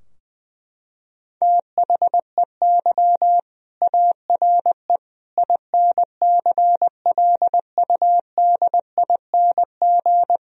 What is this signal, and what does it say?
Morse code, heard as a keyed tone, about 20 wpm, 700 Hz. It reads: THEY ARE INCLUDING
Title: Sentences from Top 1000 Words - Review Part 4 - 20wpm